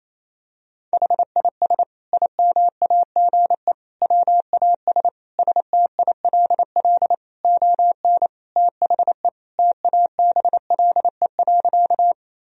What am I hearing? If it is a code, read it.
Morse code, heard as a keyed tone, about 28 wpm, 700 Hz. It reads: HIS IMAGE WAS STILL ON THE TABLE.
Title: Sentences from Top 1000 Words - Review Part 4 - 28wpm